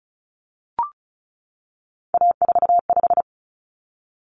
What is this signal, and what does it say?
Morse code, heard as a keyed tone, about 34 wpm, 700 Hz. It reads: A45
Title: Common Cut Numbers - 3-Digit 35wpm